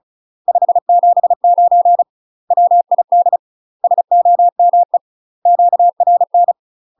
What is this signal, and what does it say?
Morse code, heard as a keyed tone, about 36 wpm, 700 Hz. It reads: 579 WID SOME QRN
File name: UK MASTER QSOs TRAINING / 35WPM / UK MASTER QSOs 0025